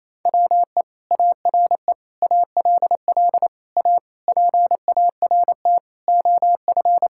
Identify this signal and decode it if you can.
Morse code, heard as a keyed tone, about 27 wpm, 700 Hz. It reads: WE ARE ALL A PART OF
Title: Sentences from Top 1000 Words - Review Part 6 - 28wpm